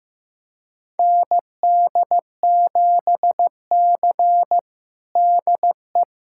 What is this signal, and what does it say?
Morse code, heard as a keyed tone, about 15 wpm, 700 Hz. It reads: ND7C DE
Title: USA Callsign DE Callsign - 4 Characters - 15wpm